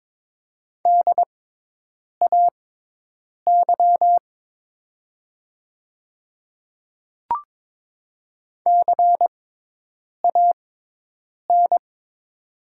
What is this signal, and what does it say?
Morse code, heard as a keyed tone, about 22 wpm, spaced as for 8 wpm, 700 Hz. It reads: DAY CAN
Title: 6x Character Spacing - Top 200 Words - 22wpm